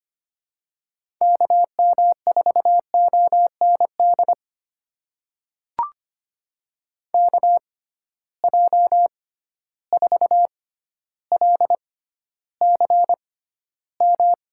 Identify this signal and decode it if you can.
Morse code, heard as a keyed tone, about 25 wpm, spaced as for 13 wpm, 700 Hz. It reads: KM4OND KJ4LCM
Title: USA Call Signs - 6 Characters - 6x Character Spacing - 25wpm